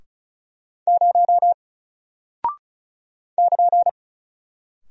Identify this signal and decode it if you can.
Morse code, heard as a keyed tone, about 36 wpm, 700 Hz. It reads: 0 <KN>
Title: Single Character (Letter:Number:Punctuation:Prosign) Alt 35wpm